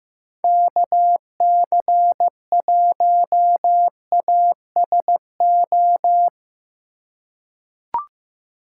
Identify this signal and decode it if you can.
Morse code, heard as a keyed tone, about 15 wpm, 700 Hz. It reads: KC1ASO
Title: USA Call Signs - 6x Character Spacing - 15wpm